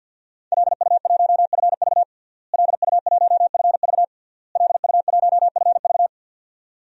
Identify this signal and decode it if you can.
Morse code, heard as a keyed tone, about 52 wpm, 700 Hz. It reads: LU1FV LU1FV LU1FV
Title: DXCC-100 Call Signs - 5 Characters - Repeated 3x - 50wpm